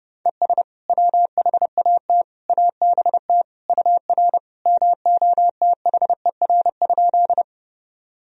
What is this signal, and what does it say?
Morse code, heard as a keyed tone, about 30 wpm, 700 Hz. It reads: ES WHAT ABT UR MOTHER?
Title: Sentences from Top 200 Words and Abbreviations 30wpm